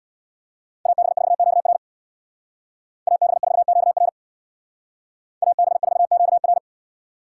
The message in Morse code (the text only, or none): AB4QR AB4QR AB4QR